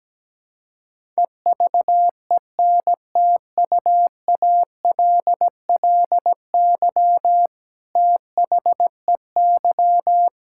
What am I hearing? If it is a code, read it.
Morse code, heard as a keyed tone, about 17 wpm, 700 Hz. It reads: EVENTUALLY THEY
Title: Sentences from Top 1000 Words - Review Part 6 - 17wpm